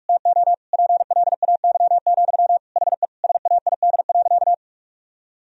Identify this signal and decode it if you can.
Morse code, heard as a keyed tone, about 46 wpm, 700 Hz. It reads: TO PLAY, HE SAID.
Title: Sentences from Top 1000 Words - Review Part 4 - 45wpm